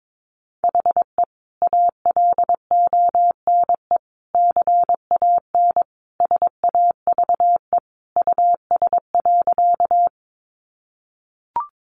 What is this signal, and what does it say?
Morse code, heard as a keyed tone, about 22 wpm, 700 Hz. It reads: HE ALONE CAN SAVE US.
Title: Sentences from Top 1100 Words 22wpm